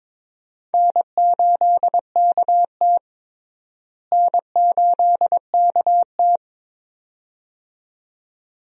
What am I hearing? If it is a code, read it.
Morse code, heard as a keyed tone, about 23 wpm, 700 Hz. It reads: N8KT N8KT
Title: USA Call Signs - 4 Characters - Repeated 2x - 22wpm